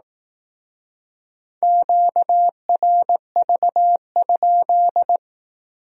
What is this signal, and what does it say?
Morse code, heard as a keyed tone, about 18 wpm, 700 Hz. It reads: QRV?
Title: Speed-Racing Q-Codes - 15wpm